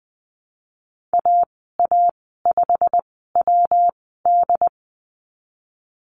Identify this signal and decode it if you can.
Morse code, heard as a keyed tone, about 20 wpm, spaced as for 15 wpm, 700 Hz. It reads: AA5WD
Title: 122 Lesson 20g - Number 3 - Call Signs 20wpm